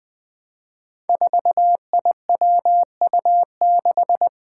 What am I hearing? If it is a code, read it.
Morse code, heard as a keyed tone, about 20 wpm, 700 Hz. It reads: 4IWU6
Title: Five Challenging Letter-Number Combinations 20wpm